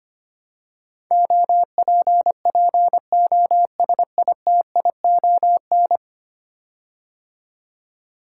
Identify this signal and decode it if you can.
Morse code, heard as a keyed tone, about 25 wpm, 700 Hz. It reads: OPPOSITION
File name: 10 Letter Words - First 3 Letters Given - 25wpm